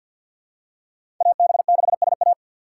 Morse code, heard as a keyed tone, about 54 wpm, 700 Hz. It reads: AB6SA
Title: USA Call Signs - 5 Characters - 8x Character Spacing - 50wpm